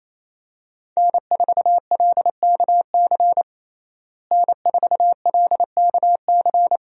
Morse code, heard as a keyed tone, about 28 wpm, 700 Hz. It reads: N4LKC N4LKC
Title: USA Call Signs - 5 Characters - Repeated 2x - 28wpm